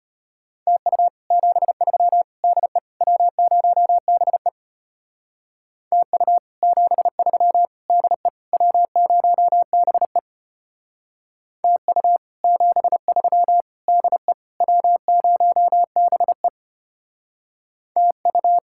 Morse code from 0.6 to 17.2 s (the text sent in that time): TU 73 DE W0BE TU 73 DE W0BE TU 73 DE W0BE